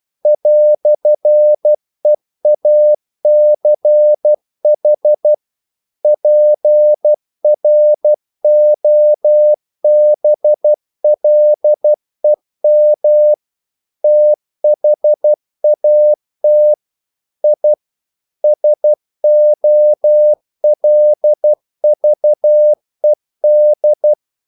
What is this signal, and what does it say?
Morse code, heard as a keyed tone, about 12 wpm, 600 Hz. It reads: "EACH PROBLEM THAT I SOLVED
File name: CW / Practice / EnglishQuotations0112